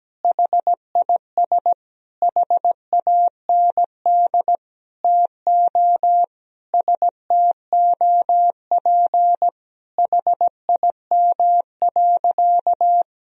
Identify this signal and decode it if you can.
Morse code, heard as a keyed tone, about 17 wpm, 700 Hz. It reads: HIS HAND TO STOP HIM.
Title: Sentences from Top 1000 Words - Review Part 4 - 17wpm